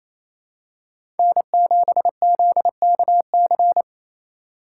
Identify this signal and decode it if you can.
Morse code, heard as a keyed tone, about 28 wpm, 700 Hz. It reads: N7ZKC
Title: USA Call Signs - 3x Character Spacing - 28wpm